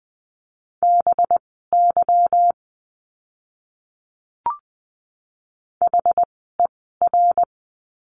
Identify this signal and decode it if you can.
Morse code, heard as a keyed tone, about 20 wpm, spaced as for 15 wpm, 700 Hz. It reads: BY HER
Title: 244 Lesson 39f - Letter z - Words 20wpm